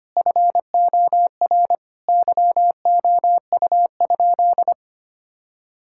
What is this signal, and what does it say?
Morse code, heard as a keyed tone, about 25 wpm, 700 Hz. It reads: FOR YOU?
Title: Sentences from Top 1000 Words - Review Part 6 - 25wpm